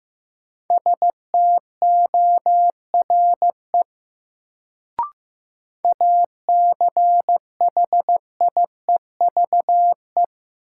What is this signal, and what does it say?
Morse code, heard as a keyed tone, about 15 wpm, 700 Hz. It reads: STORE ACHIEVE
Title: Sentences from Top 1000 Words - Review Part 5 - New Word Drill - 15wpm